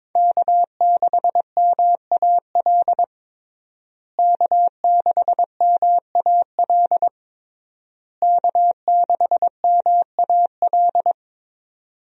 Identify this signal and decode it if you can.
Morse code, heard as a keyed tone, about 22 wpm, 700 Hz. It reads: K6MAL K6MAL K6MAL
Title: USA Call Signs - 5 Characters - Repeated 3x - 22wpm